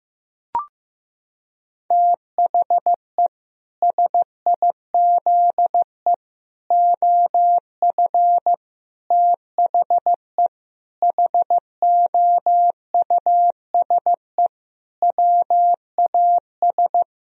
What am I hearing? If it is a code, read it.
Morse code, heard as a keyed tone, about 15 wpm, 700 Hz. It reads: THE SIZE OF THE HOUSE WAS
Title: Sentences from Top 1000 Words - Review Part 2 - 15wpm